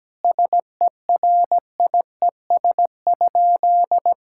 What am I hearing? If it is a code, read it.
Morse code, heard as a keyed tone, about 17 wpm, 700 Hz. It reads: SERIES?
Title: Sentences from Top 1000 Words - Review Part 5 - 17wpm